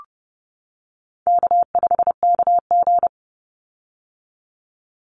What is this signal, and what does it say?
Morse code, heard as a keyed tone, about 30 wpm, 700 Hz. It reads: K5KG
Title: Call Signs - 30wpm